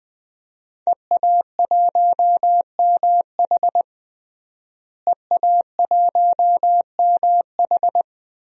Morse code, heard as a keyed tone, about 20 wpm, 700 Hz. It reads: EA1MH EA1MH
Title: DXCC-100 Call Signs - 5 Characters - Repeated 3x - 20wpm